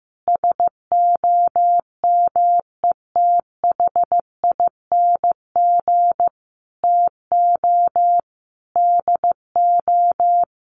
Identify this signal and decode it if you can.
Morse code, heard as a keyed tone, about 15 wpm, 700 Hz. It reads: SOMETHING TO DO
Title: Sets of 5 Words Encore 15wpm